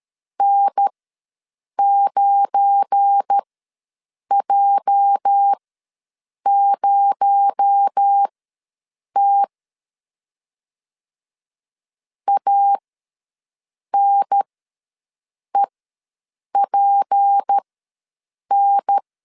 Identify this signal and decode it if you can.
Morse code, heard as a keyed tone, about 13 wpm, spaced as for 7 wpm, 800 Hz. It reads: N9J0T ANEPN
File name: LEZIONI / 8 YJ45 / 8 YJ ws 12wpm 3342